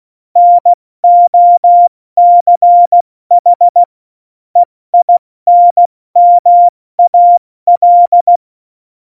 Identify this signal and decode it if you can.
Morse code, heard as a keyed tone, about 16 wpm, spaced as for 14 wpm, 700 Hz. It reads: NOCH EINMAL
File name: Morsekurs / Witze / Witz 25. Juni 2019, 70 F80